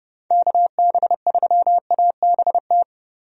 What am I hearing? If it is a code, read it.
Morse code, heard as a keyed tone, about 30 wpm, 700 Hz. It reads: KB3ABT
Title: USA Call Signs - 6 Characters - 30wpm